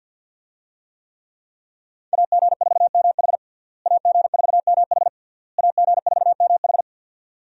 Morse code, heard as a keyed tone, about 52 wpm, 700 Hz. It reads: AG4MH AG4MH AG4MH
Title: USA Call Signs - 5 Characters - Repeated 3x - 50wpm